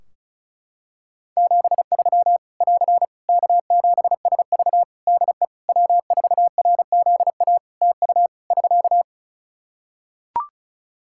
Morse code, heard as a keyed tone, about 34 wpm, 700 Hz. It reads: 73 <AR> K7SV DE W4RZA TU <SK>
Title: Practice QSO - Exchange 4 - Ending - 35wpm